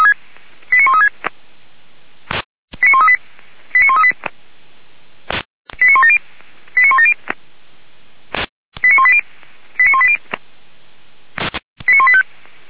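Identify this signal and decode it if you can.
Five-tone selective call (ZVEI), heard as quick runs of five tones, tones of about 70 ms. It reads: 89136 89138 89138 89170 89170 89180 89180 89185
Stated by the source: ZVEI five-tone selective call 89135 received by SDR, from the Feuerwehr Orlamünde fire brigade audio server.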